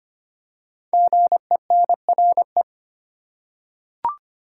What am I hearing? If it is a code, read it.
Morse code, heard as a keyed tone, about 25 wpm, 700 Hz. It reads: GENRE
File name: 5 Letter Words - First 3 Letters Given - 25wpm